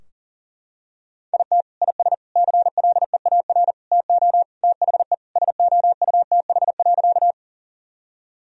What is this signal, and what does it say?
Morse code, heard as a keyed tone, about 40 wpm, 700 Hz. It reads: IT IS CLEAR TO THE SOUTH.
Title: Sentences from Top 1000 Words - Review Part 2 - 40wpm